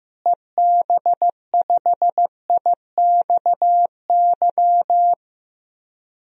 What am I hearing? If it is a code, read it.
Morse code, heard as a keyed tone, about 15 wpm, 700 Hz. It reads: EB5IXY
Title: DXCC-100 Call Signs - 6 Characters - Repeated 3x - 15wpm